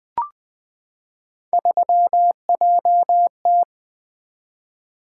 Morse code, heard as a keyed tone, about 20 wpm, 700 Hz. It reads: 3JT
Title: Three Letter-Number Combinations - 20wpm